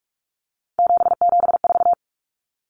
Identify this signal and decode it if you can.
Morse code, heard as a keyed tone, about 44 wpm, 700 Hz. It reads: 774
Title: Common Cut Numbers - 3-Digit 45wpm